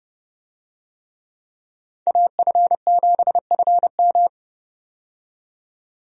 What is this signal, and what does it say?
Morse code, heard as a keyed tone, about 29 wpm, 700 Hz. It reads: AF7FM
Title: USA Call Signs - 5 Characters - 30wpm